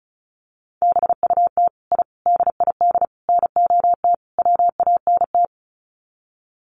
Morse code, heard as a keyed tone, about 34 wpm, 700 Hz. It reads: BUT I DID NOT WANT
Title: Sets of 5 Words Encore 35wpm